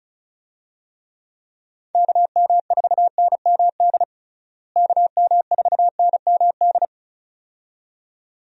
Morse code, heard as a keyed tone, about 36 wpm, 700 Hz. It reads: KM4NMD KM4NMD
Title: USA Call Signs - 6 Characters - Repeated 2x - 35wpm